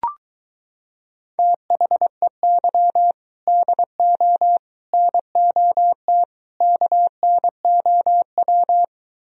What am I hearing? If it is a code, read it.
Morse code, heard as a keyed tone, about 23 wpm, 700 Hz. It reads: THEY DO NOT KNOW